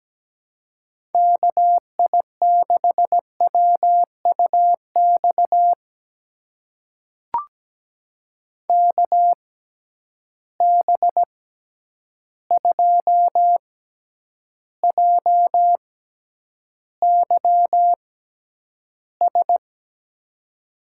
Morse code, heard as a keyed tone, about 17 wpm, spaced as for 9 wpm, 700 Hz. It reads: KI6WUX KB2JYS
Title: USA Call Signs - 6 Characters - 6x Character Spacing - 17wpm